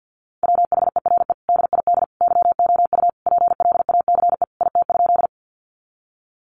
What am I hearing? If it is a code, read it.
Morse code, heard as a keyed tone, about 50 wpm, 700 Hz. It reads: WHERE DID YOU PLACE IT?
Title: Sentences from Top 200 Words 50wpm